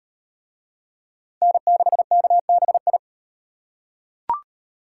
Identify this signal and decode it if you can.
Morse code, heard as a keyed tone, about 35 wpm, 700 Hz. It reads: N6KBI